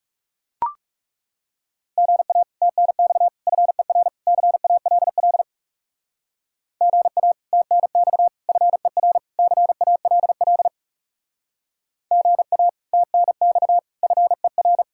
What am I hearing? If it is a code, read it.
Morse code, heard as a keyed tone, about 37 wpm, 700 Hz. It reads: GA TNX FER CALL GA TNX FER CALL GA TNX FER
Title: Speed-Racing Common QSO Phrases - 30wpm